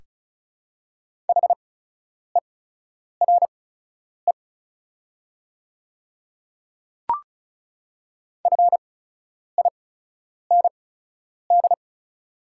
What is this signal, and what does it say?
Morse code, heard as a keyed tone, about 36 wpm, spaced as for 10 wpm, 700 Hz. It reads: HERE FIND
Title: 8x Character Spacing - Top 300 Words - 35wpm